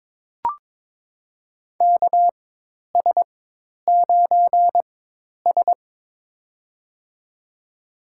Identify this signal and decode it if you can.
Morse code, heard as a keyed tone, about 22 wpm, spaced as for 10 wpm, 700 Hz. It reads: KS9S
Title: USA Call Signs - 4 Characters - 4x Character Spacing - 22wpm